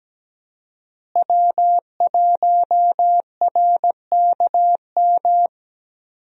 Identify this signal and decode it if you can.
Morse code, heard as a keyed tone, about 17 wpm, 700 Hz. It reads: W1RKM
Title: USA Call Signs - 5 Characters - Repeated 2x - 17wpm